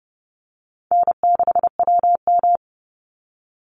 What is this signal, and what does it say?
Morse code, heard as a keyed tone, about 30 wpm, 700 Hz. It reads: N6WM
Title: Easy Call Signs - 30wpm